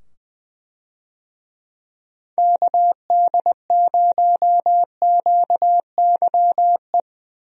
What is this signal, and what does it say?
Morse code, heard as a keyed tone, about 20 wpm, 700 Hz. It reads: KD0QYE
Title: USA Call Signs - 6 Characters - 20wpm